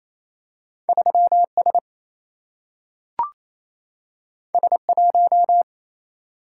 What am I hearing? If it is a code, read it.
Morse code, heard as a keyed tone, about 28 wpm, 700 Hz. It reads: 3 S S 1
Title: Single-Letter Single-Number 28wpm